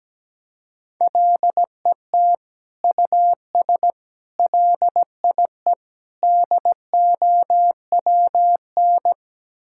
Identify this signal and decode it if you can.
Morse code, heard as a keyed tone, about 17 wpm, 700 Hz. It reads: LET US LIE DOWN
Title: Sentences from Top 1600 Words 17wpm